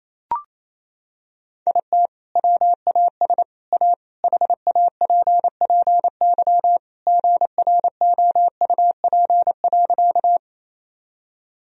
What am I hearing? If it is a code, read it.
Morse code, heard as a keyed tone, about 28 wpm, 700 Hz. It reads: IT WAS A HAPPY GROUP.
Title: Sentences from Top 1000 Words - Review Part 1 - 28wpm